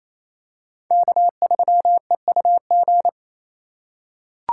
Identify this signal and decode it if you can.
Morse code, heard as a keyed tone, about 28 wpm, 700 Hz. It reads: K3EUG
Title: USA Call Signs - 4x Character Spacing - 28wpm